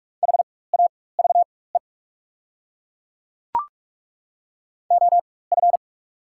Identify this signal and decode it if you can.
Morse code, heard as a keyed tone, about 42 wpm, spaced as for 21 wpm, 700 Hz. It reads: HAVE OF
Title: 4x Character Spacing - Top 300 Words - 45wpm